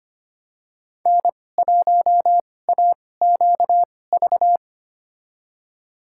Morse code, heard as a keyed tone, about 25 wpm, spaced as for 18 wpm, 700 Hz. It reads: N1AQV
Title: USA Call Signs - 5 Characters - 2x Character Spacing - 25wpm